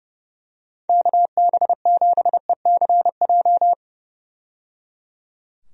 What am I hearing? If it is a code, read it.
Morse code, heard as a keyed tone, about 30 wpm, 700 Hz. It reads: KB7ECJ
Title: USA Call Signs - 6 Characters - 30wpm